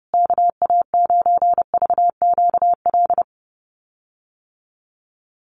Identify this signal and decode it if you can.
Morse code, heard as a keyed tone, about 30 wpm, 700 Hz. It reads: KA9VQL
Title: USA Call Signs Repeated 3x 30wpm